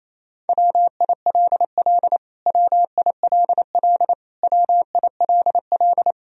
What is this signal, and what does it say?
Morse code, heard as a keyed tone, about 28 wpm, 700 Hz. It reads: WILL WILL WILL
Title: Top 100 Words Repeated 28wpm